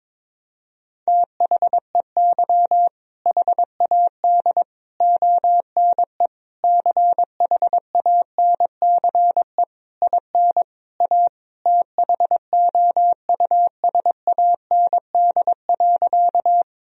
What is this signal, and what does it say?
Morse code, heard as a keyed tone, about 22 wpm, 700 Hz. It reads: THEY HAD ONE CHANCE IN A THOUSAND.
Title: Sentences from Top 1000 Words - Review Part 3 - 22wpm